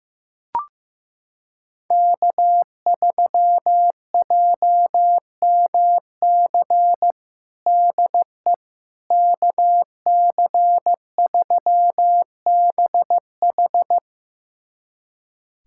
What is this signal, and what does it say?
Morse code, heard as a keyed tone, about 15 wpm, 700 Hz. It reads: K3JMC DE KC3BH
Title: USA Callsign DE Callsign - 5 Characters - 15wpm